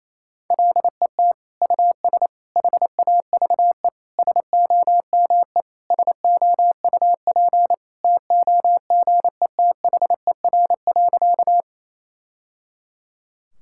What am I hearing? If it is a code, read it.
Morse code, heard as a keyed tone, about 27 wpm, 700 Hz. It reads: LET US HAVE SOME SOUP TOGETHER.
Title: Sentences from Top 1000 Words - Review Part 6 - 28wpm